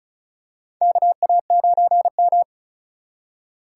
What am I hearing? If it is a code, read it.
Morse code, heard as a keyed tone, about 36 wpm, 700 Hz. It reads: KA9M
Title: USA Call Signs - 4 Characters - 6x Character Spacing - 35wpm